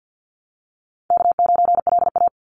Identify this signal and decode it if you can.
Morse code, heard as a keyed tone, about 50 wpm, 700 Hz. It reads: K9LA
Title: Easy Call Signs - 50wpm